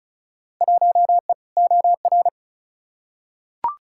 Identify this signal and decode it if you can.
Morse code, heard as a keyed tone, about 36 wpm, 700 Hz. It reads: 1E OR
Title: Field Day Exchanges - 35wpm